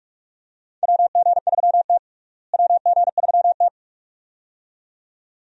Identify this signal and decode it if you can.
Morse code, heard as a keyed tone, about 42 wpm, 700 Hz. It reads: WG3T WG3T
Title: USA Call Signs - 4 Characters - Repeated 2x - 45wpm